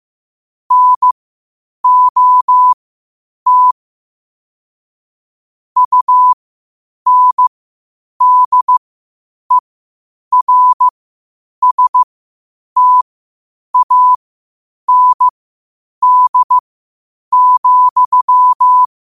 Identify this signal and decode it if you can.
Morse code, heard as a keyed tone, about 15 wpm, spaced as for 8 wpm, 1000 Hz. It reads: NOT UNDERSTAND,
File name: qotdfiles / 8wpm